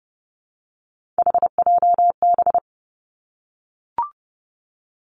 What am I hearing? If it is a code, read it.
Morse code, heard as a keyed tone, about 30 wpm, 700 Hz. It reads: HJB
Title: Three Challenging Letter-Number Combinations 30wpm